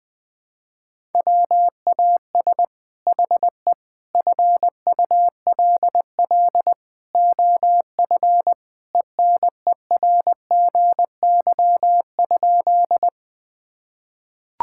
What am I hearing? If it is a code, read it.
Morse code, heard as a keyed tone, about 21 wpm, 700 Hz. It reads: WAS HE FULL OF ENERGY?
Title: Sentences from Top 1000 Words - Review Part 6 - 20wpm